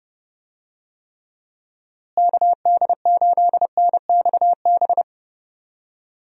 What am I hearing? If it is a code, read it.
Morse code, heard as a keyed tone, about 30 wpm, 700 Hz. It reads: KD8NXB
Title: USA Call Signs - 6 Characters - Repeated 3x - 30wpm